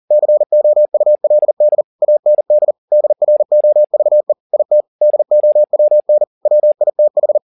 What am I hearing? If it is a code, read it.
Morse code, heard as a keyed tone, about 40 wpm, 600 Hz. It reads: COULD AND DROVE IT DOWN WITH
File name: the old man and the sea 2 40wpm